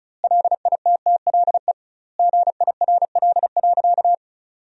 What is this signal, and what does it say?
Morse code, heard as a keyed tone, about 36 wpm, 700 Hz. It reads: LITTLE GIRL.
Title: 2x Word-Spacing - Sentences from Top 400 Words - 35wpm